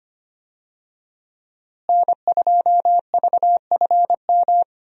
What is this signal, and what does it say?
Morse code, heard as a keyed tone, about 25 wpm, 700 Hz. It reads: N2VFM
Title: USA Call Signs - 5 Characters - Repeated 3x - 25wpm